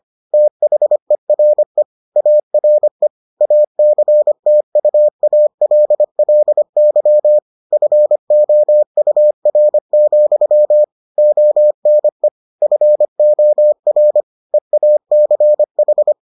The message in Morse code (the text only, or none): THERE ARE ACTUALLY FOUR, ONE FOR EACH